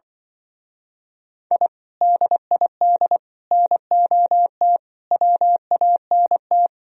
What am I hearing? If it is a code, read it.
Morse code, heard as a keyed tone, about 25 wpm, 700 Hz. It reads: I DID NOT WANT
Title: Speed-Racing - Sets of 5 Words 20wpm